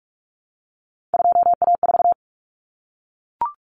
Morse code, heard as a keyed tone, about 44 wpm, 700 Hz. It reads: 2A4
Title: Common Cut Numbers - 3-Digit 45wpm